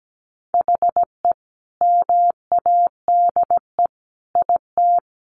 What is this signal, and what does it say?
Morse code, heard as a keyed tone, about 17 wpm, 700 Hz. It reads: HE MADE IT
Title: Sentences from Top 1300 Words 17wpm